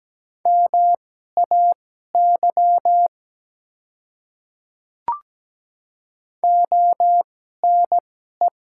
Morse code, heard as a keyed tone, about 17 wpm, spaced as for 12 wpm, 700 Hz. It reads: MAY ONE
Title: 2x Character Spacing - Top 200 Words - 17wpm